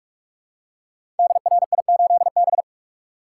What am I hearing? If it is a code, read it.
Morse code, heard as a keyed tone, about 44 wpm, 700 Hz. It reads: DRI8B
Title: Code Groups - 5 Letter:Numbers - 45wpm